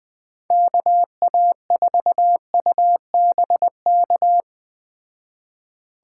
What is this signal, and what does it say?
Morse code, heard as a keyed tone, about 21 wpm, 700 Hz. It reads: KA4UBK
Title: Speed Racing - USA 2x3 Call Signs 17wpm